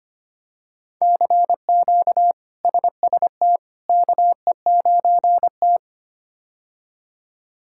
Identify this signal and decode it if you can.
Morse code, heard as a keyed tone, about 25 wpm, 700 Hz. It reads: CQ SST KE9T
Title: K1USN SST QSO Practice - 25wpm